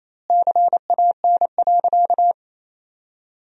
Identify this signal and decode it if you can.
Morse code, heard as a keyed tone, about 28 wpm, 700 Hz. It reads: CAN.
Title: 2x Word-Spacing - Sentences from Top 500 Words - 28wpm